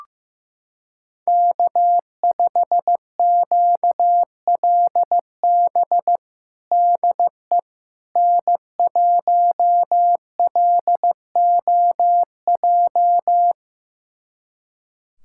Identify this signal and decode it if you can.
Morse code, heard as a keyed tone, about 15 wpm, 700 Hz. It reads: K5QLB DE N1LOJ
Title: USA Callsign DE Callsign - 5 Characters - 15wpm